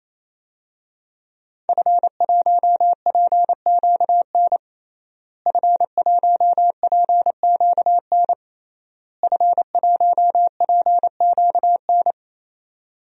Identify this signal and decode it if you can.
Morse code, heard as a keyed tone, about 28 wpm, 700 Hz. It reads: F1PQN F1PQN F1PQN
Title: DXCC-100 Call Signs - 5 Characters - Repeated 3x - 28wpm